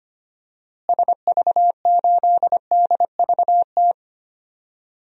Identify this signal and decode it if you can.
Morse code, heard as a keyed tone, about 25 wpm, 700 Hz. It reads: SV8DVT
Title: DXCC-100 Call Signs - 6 Characters - Repeated 3x - 25wpm